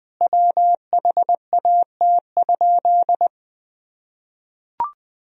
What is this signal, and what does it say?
Morse code, heard as a keyed tone, about 21 wpm, 700 Hz. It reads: WHAT?
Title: Sentences from Top 100 Words and Abbreviations 20wpm